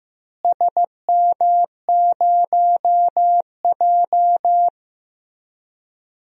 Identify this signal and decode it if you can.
Morse code, heard as a keyed tone, about 15 wpm, 700 Hz. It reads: SM0J
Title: DXCC-100 Call Signs - 4 Characters - Repeated 3x - 15wpm